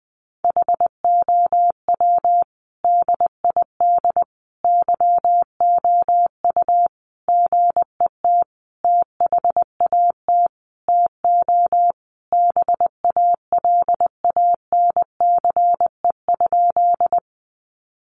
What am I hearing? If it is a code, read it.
Morse code, heard as a keyed tone, about 20 wpm, 700 Hz. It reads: HOW DID YOU GET THAT TO BALANCE?
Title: Sentences from Top 900 Words 20wpm